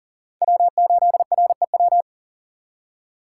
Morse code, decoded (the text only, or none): W8REW